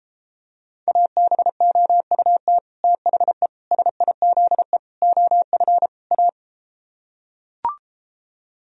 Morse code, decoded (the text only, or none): ABOUT THE SIZE OF A